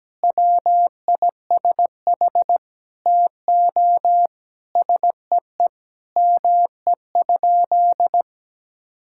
Morse code, heard as a keyed tone, about 17 wpm, 700 Hz. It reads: WISH TO SEE ME?
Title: Sentences from Top 1600 Words 17wpm